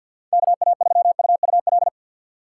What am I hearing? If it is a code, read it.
Morse code, heard as a keyed tone, about 51 wpm, 700 Hz. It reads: KA3UUL
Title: USA Call Signs - 6x Character Spacing - 50wpm